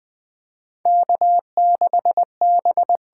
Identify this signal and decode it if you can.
Morse code, heard as a keyed tone, about 20 wpm, 700 Hz. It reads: K6B
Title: USA Call Signs - 8x Character Spacing - 20wpm